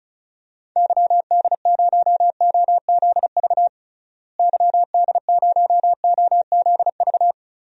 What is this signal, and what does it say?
Morse code, heard as a keyed tone, about 34 wpm, 700 Hz. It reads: YD0OZV YD0OZV
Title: DXCC-100 Call Signs - 6 Characters - Repeated 2x - 35wpm